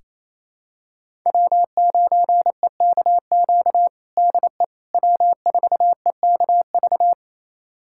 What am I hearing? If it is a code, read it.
Morse code, heard as a keyed tone, about 27 wpm, 700 Hz. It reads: W9EKQ DE W4EKV
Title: USA Callsign DE Callsign - 5 Characters - 28wpm